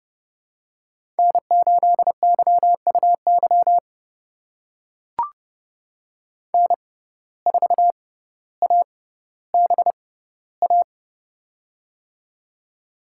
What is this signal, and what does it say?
Morse code, heard as a keyed tone, about 32 wpm, spaced as for 16 wpm, 700 Hz. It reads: N8YUY N4ABA
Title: USA Call Signs - 5 Characters - 6x Character Spacing - 30wpm